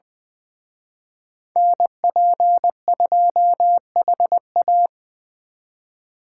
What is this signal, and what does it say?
Morse code, heard as a keyed tone, about 23 wpm, 700 Hz. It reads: NP2HA
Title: Speed Racing - USA 2x2 Call Signs 20wpm